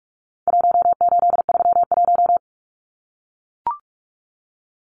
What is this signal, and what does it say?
Morse code, heard as a keyed tone, about 44 wpm, 700 Hz. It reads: 1831
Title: Numbers - Contest Style - 4-Digit 45wpm